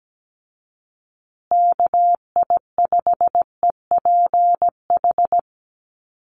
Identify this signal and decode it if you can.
Morse code, heard as a keyed tone, about 17 wpm, 700 Hz. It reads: KI5EPH
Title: USA Call Signs Repeated 2x 17wpm